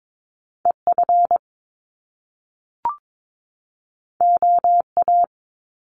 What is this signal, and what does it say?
Morse code, heard as a keyed tone, about 22 wpm, 700 Hz. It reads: EF OA
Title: Top Two Letter Combinations in Words 22wpm